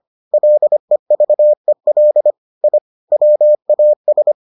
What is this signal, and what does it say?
Morse code, heard as a keyed tone, about 25 wpm, 600 Hz. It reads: LEVEL I WAS